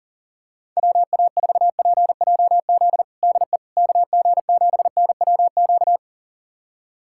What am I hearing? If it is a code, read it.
Morse code, heard as a keyed tone, about 40 wpm, 700 Hz. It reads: WA4PJZ DE KG7NWQ
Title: USA Callsign DE Callsign 40wpm